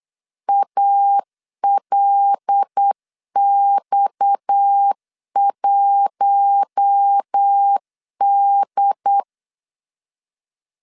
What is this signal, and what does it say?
Morse code, heard as a keyed tone, about 8 wpm, 800 Hz. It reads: ALX1D
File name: LEZIONI / 6 PKX0 / 6 PKX 8wpm 3041